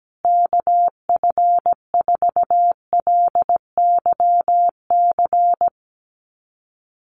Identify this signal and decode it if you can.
Morse code, heard as a keyed tone, about 17 wpm, 700 Hz. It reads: KF4LYC
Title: USA Call Signs Repeated 2x 17wpm